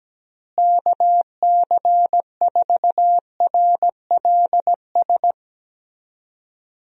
Morse code, heard as a keyed tone, about 17 wpm, 700 Hz. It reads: KC4RLS
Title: USA Call Signs - 6 Characters - Repeated 2x - 17wpm